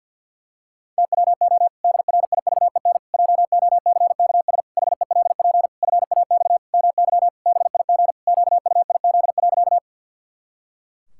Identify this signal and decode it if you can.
Morse code, heard as a keyed tone, about 52 wpm, 700 Hz. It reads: TWO DRIVEN JOCKS HELP FAX MY BIG QUIZ.
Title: Pangram Challenge v2 - 50wpm